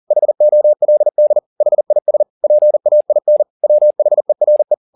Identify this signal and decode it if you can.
Morse code, heard as a keyed tone, about 40 wpm, 600 Hz. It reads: HOLD HIS PAIN WHERE
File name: the old man and the sea 2 40wpm